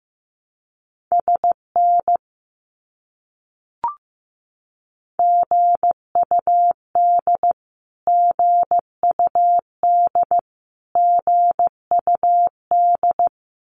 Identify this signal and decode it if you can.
Morse code, heard as a keyed tone, about 15 wpm, 700 Hz. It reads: SN GUD GUD GUD
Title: Abbreviations Repeated 3x - 15wpm